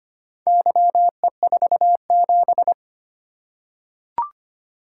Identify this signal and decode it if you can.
Morse code, heard as a keyed tone, about 25 wpm, 700 Hz. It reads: YE47
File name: practice / 25wpm / Four Letter-Number Combinations - 25wpm